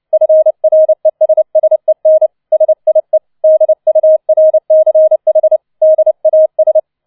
Morse code, heard as a keyed tone, about 29 wpm, 630 Hz. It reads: FRESSEN SIE DURCH DAS
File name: 006 Sechster Streich 30wpm